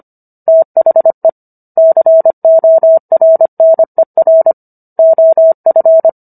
Morse code, heard as a keyed tone, about 25 wpm, 650 Hz. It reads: THE CORNER OF